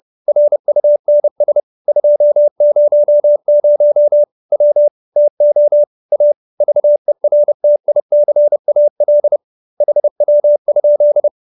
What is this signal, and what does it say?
Morse code, heard as a keyed tone, about 30 wpm, 590 Hz. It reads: RUNS 200 W TO A VERTICAL HW?